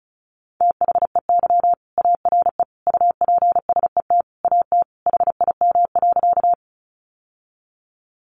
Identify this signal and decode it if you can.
Morse code, heard as a keyed tone, about 34 wpm, 700 Hz. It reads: THEY ARE UPSET AT HIM.
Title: Sentences from Top 800 Words 35wpm